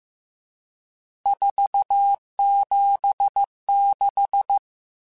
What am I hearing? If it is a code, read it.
Morse code, heard as a keyed tone, about 15 wpm, 800 Hz. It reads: 476
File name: RSTs / RST-mp3s / Various RST @13